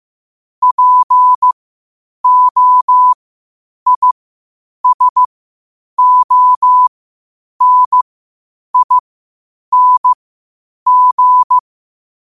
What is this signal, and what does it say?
Morse code, heard as a keyed tone, about 15 wpm, spaced as for 8 wpm, 1000 Hz. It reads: POISONING
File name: qotdfiles / 8wpm